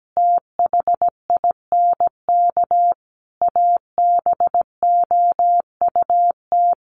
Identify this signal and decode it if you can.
Morse code, heard as a keyed tone, about 17 wpm, 700 Hz. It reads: THINK ABOUT
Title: Sentences from Top 200 Words 17wpm